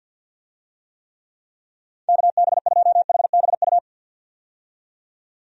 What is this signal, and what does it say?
Morse code, heard as a keyed tone, about 49 wpm, 700 Hz. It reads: KB2HBU
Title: USA Call Signs - 6 Characters - 50wpm